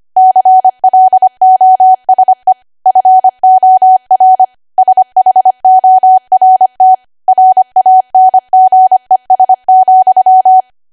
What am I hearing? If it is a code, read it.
Morse code, heard as a keyed tone, about 25 wpm, 750 Hz. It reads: CLOSE FOR SHORT RANGES,
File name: Archive / 25 WPM / 25WPM